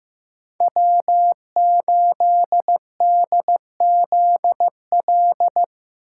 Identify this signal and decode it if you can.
Morse code, heard as a keyed tone, about 17 wpm, 700 Hz. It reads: W8DZL
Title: Speed Racing - USA 1x3 Call Signs 15wpm